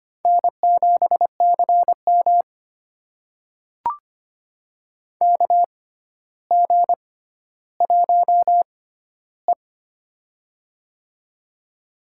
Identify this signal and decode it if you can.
Morse code, heard as a keyed tone, about 25 wpm, spaced as for 13 wpm, 700 Hz. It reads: N7CM KG1E
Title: USA Call Signs - 6x Character Spacing - 25wpm